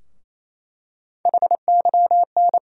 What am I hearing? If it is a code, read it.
Morse code, heard as a keyed tone, about 28 wpm, 700 Hz. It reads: HYN
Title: USA Call Sign Postfixes - 3 Letters 28wpm